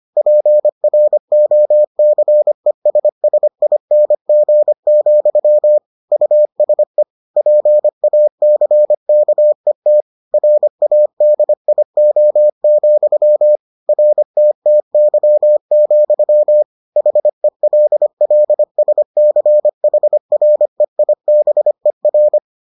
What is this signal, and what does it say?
Morse code, heard as a keyed tone, about 25 wpm, 600 Hz. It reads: PROCESSING, USE PACKET RADIO, RTTY, HELLSCHREIBER